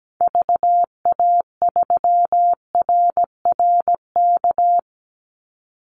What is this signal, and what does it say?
Morse code, heard as a keyed tone, about 17 wpm, 700 Hz. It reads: VA3RRK
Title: Call Signs - 17wpm